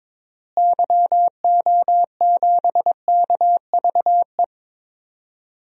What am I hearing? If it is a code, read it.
Morse code, heard as a keyed tone, about 22 wpm, 700 Hz. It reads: YO7KVE
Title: DXCC-100 Call Signs - 6 Characters - Repeated 3x - 22wpm